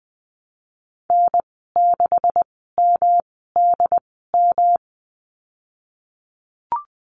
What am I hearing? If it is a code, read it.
Morse code, heard as a keyed tone, about 20 wpm, spaced as for 15 wpm, 700 Hz. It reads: N6MDM
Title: 128 Lesson 21g - Number 6 - Call Signs 20wpm